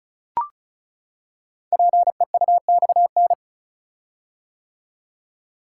Morse code, heard as a keyed tone, about 36 wpm, 690 Hz. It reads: PEUXN
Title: Code Groups - 5 Letters - 35wpm